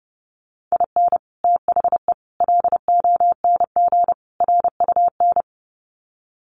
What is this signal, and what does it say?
Morse code, heard as a keyed tone, about 30 wpm, 700 Hz. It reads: IN THE LONG RUN
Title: Sets of 4 Words 30wpm